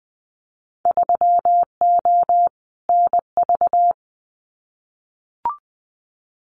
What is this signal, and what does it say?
Morse code, heard as a keyed tone, about 20 wpm, 700 Hz. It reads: 3O NV
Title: Winter Field Day Exchanges Encore 20wpm